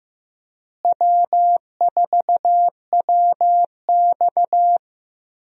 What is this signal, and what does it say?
Morse code, heard as a keyed tone, about 15 wpm, 700 Hz. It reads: W4WX
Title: USA Call Signs - 4 Characters - Repeated 3x - 15wpm